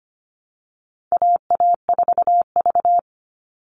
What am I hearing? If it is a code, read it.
Morse code, heard as a keyed tone, about 25 wpm, 700 Hz. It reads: AA4V
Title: Call Signs Repeated 3x - 25wpm